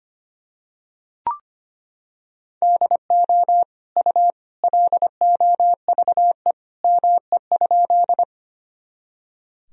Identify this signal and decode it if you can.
Morse code, heard as a keyed tone, about 25 wpm, 700 Hz. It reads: DO U LOVE ME?
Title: Sentences from Top 400 Words and Abbreviations 25wpm